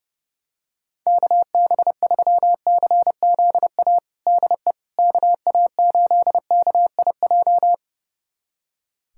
Morse code, heard as a keyed tone, about 30 wpm, 700 Hz. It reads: KB3CZA DE KA8KIJ
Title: USA Callsign DE Callsign - 6 Characters - 30wpm